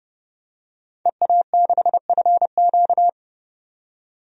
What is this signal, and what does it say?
Morse code, heard as a keyed tone, about 30 wpm, 700 Hz. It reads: EA6FQ
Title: DXCC-100 Call Signs - 5 Characters - 30wpm